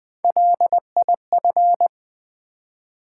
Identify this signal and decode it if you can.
Morse code, heard as a keyed tone, about 20 wpm, 700 Hz. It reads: LIF
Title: Build A Word - 1 Letter At A Time - 3 to 5-Letter Words - 20wpm